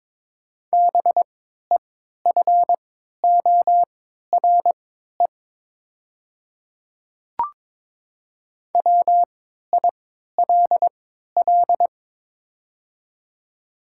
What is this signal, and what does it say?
Morse code, heard as a keyed tone, about 22 wpm, spaced as for 12 wpm, 700 Hz. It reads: BEFORE WILL
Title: 3x Character Spacing - Top 200 Words - 22wpm